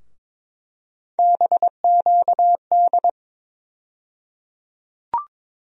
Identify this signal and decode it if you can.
Morse code, heard as a keyed tone, about 22 wpm, 700 Hz. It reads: BQD
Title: USA Call Sign Postfixes - 3 Letters 22wpm